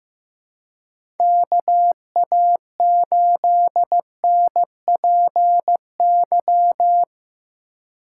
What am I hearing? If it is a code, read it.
Morse code, heard as a keyed tone, about 15 wpm, 700 Hz. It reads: KA8NPY
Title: USA Call Signs - 4x Character Spacing - 15wpm